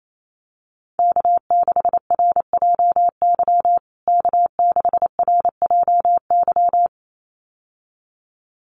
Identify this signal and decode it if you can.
Morse code, heard as a keyed tone, about 28 wpm, 700 Hz. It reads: K6RJY K6RJY
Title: USA Call Signs Repeated 2x 28wpm